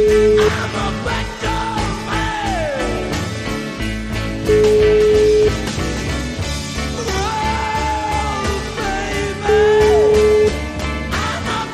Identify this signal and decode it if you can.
Ringback tone (Europe, 425 Hz), with noise.